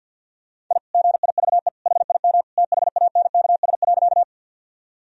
Morse code, heard as a keyed tone, about 51 wpm, 700 Hz. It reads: I GIVE HIM THANKS.
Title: Sentences from Top 1000 Words - Review Part 3 - 50wpm